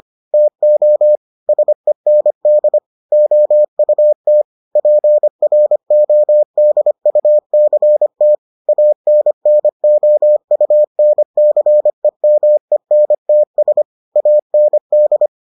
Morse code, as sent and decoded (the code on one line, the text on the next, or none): - --- / ... -. -.. / --- ..- - / .--. .-. --- -.. ..- -.-. - / .- -. -. --- ..- -. -.-. -- . -. - ... / .- -. -..
TO SEND OUT PRODUCT ANNOUNCEMENTS AND